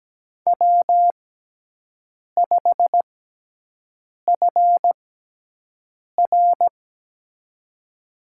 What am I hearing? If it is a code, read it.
Morse code, heard as a keyed tone, about 17 wpm, spaced as for 6 wpm, 700 Hz. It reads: W5FR
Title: USA Call Signs - 6x Character Spacing - 17wpm